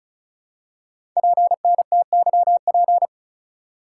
Morse code, heard as a keyed tone, about 36 wpm, 700 Hz. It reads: PNTYP
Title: Code Groups - 5 Letter:Numbers - 35wpm